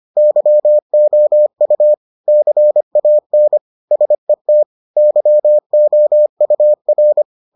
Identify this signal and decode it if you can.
Morse code, heard as a keyed tone, about 25 wpm, 600 Hz. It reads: YOU CAN SET YOUR